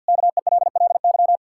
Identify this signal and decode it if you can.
Morse code, heard as a keyed tone, about 47 wpm, 700 Hz. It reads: KELLY
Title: K1USN SST QSO Practice 50wpm